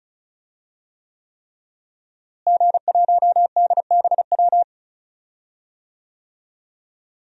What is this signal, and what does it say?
Morse code, heard as a keyed tone, about 36 wpm, 700 Hz. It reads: G1DBW
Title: DXCC-100 Call Signs - 5 Characters - Repeated 3x - 35wpm